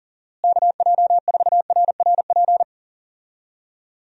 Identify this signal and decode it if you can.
Morse code, heard as a keyed tone, about 40 wpm, 700 Hz. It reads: KJ4RRP
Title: USA Call Signs - 6 Characters - Repeated 2x - 40wpm